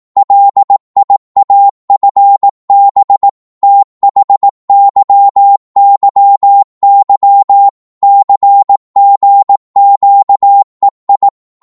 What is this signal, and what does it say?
Morse code, heard as a keyed tone, about 18 wpm, 800 Hz. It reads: LIAFB THYYY CGQEI